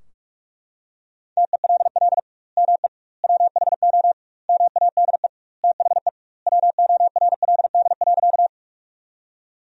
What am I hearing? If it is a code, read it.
Morse code, heard as a keyed tone, about 44 wpm, 700 Hz. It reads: TELL ME WHO MADE THE WORLD.
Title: 2x Word-Spacing - Sentences from Top 200 Words - 45wpm